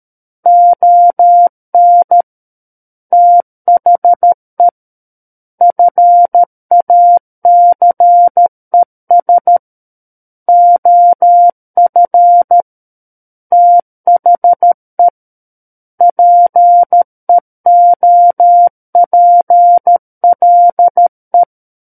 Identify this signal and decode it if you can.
Morse code, heard as a keyed tone, about 13 wpm, 700 Hz. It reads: ON THE FACES OF THE PEOPLE
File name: Book One / 04-Chapter 4 - 13 WPM